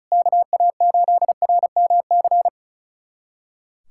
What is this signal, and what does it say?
Morse code, heard as a keyed tone, about 34 wpm, 700 Hz. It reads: KA8RMC
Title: USA Callsign DE Callsign - 6 Characters - 35wpm